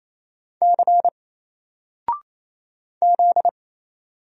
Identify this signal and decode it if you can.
Morse code, heard as a keyed tone, about 28 wpm, 700 Hz. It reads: C Z
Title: Single Character (Letter:Number:Punctuation:Prosign) Alt 28wpm